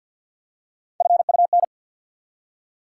Morse code, heard as a keyed tone, about 53 wpm, 700 Hz. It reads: FUN
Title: Three Letter Words - 50wpm